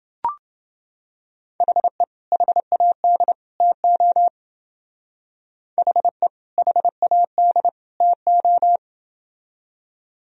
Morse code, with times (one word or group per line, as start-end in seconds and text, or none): 1.6-2.0 HE
2.3-3.3 HAD
3.6-4.3 TO
5.8-6.3 HE
6.6-7.7 HAD
8.0-8.8 TO